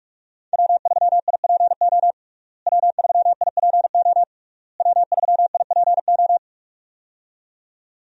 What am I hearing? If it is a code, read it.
Morse code, heard as a keyed tone, about 46 wpm, 700 Hz. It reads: W3IPO W3IPO W3IPO
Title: USA Call Signs - 5 Characters - Repeated 3x - 45wpm